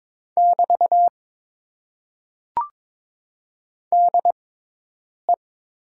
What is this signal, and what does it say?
Morse code, heard as a keyed tone, about 22 wpm, spaced as for 8 wpm, 700 Hz. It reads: <BT> DE
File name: Top 90 QSO Elements - 6x Character Spacing - 22wpm